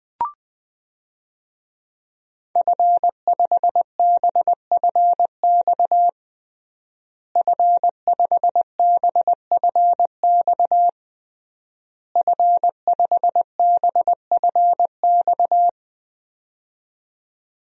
Morse code, heard as a keyed tone, about 20 wpm, 700 Hz. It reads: F5BFX F5BFX F5BFX
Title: DXCC-100 Call Signs - 5 Characters - Repeated 3x - 20wpm